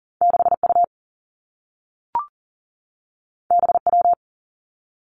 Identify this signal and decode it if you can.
Morse code, heard as a keyed tone, about 42 wpm, 700 Hz. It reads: 6 U B W
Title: Two Challenging Letter-Number Combinations 40wpm